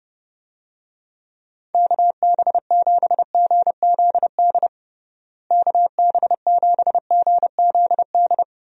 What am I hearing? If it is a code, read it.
Morse code, heard as a keyed tone, about 32 wpm, 700 Hz. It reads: KB7GZD KB7GZD
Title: USA Call Signs - 6 Characters - Repeated 2x - 30wpm